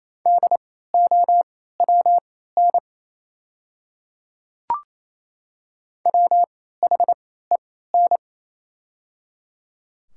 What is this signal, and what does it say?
Morse code, heard as a keyed tone, about 28 wpm, spaced as for 16 wpm, 700 Hz. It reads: DOWN WHEN
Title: 3x Character Spacing - Top 200 Words - 28wpm